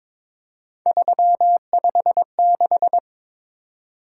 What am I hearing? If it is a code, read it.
Morse code, heard as a keyed tone, about 23 wpm, 700 Hz. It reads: 356
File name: practice / 22wpm / Three Letter-Number Combinations - 22wpm